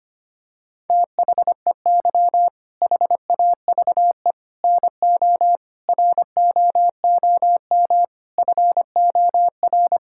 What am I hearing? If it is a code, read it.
Morse code, heard as a keyed tone, about 25 wpm, 700 Hz. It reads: THEY HAVE NO ROOM FOR
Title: Sentences from Top 1000 Words - Review Part 5 - 25wpm